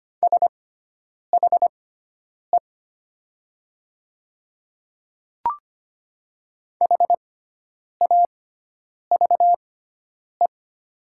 Morse code, heard as a keyed tone, about 25 wpm, spaced as for 9 wpm, 700 Hz. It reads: SHE HAVE
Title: 6x Character Spacing - Top 100 Words - 25wpm